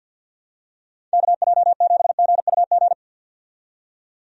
Morse code, heard as a keyed tone, about 50 wpm, 700 Hz. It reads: KJ7GUG
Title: USA Call Signs - 6 Characters - 50wpm